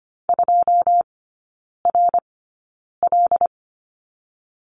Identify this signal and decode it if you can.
Morse code, heard as a keyed tone, about 25 wpm, spaced as for 9 wpm, 690 Hz. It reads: 2RL
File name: Lesson 14e - Number 2 - Three Characters 25wpm